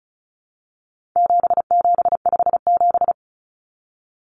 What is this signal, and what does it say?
Morse code, heard as a keyed tone, about 34 wpm, 700 Hz. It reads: 7757
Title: Numbers - 4-Digit 35wpm